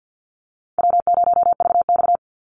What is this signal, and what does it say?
Morse code, heard as a keyed tone, about 50 wpm, 700 Hz. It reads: W0VX
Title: Call Signs - 50wpm